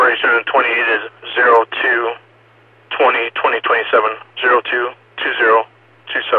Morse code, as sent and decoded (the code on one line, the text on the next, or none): none